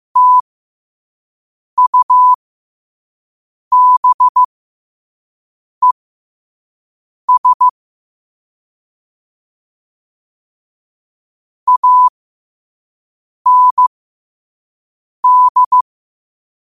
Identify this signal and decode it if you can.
Morse code, heard as a keyed tone, about 15 wpm, spaced as for 5 wpm, 1000 Hz. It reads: TUBES AND
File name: qotdfiles / 5wpm